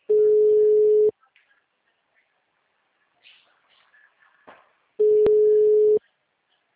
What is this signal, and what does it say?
Ringback tone (Europe, 425 Hz).
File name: out- -2010- - - .98302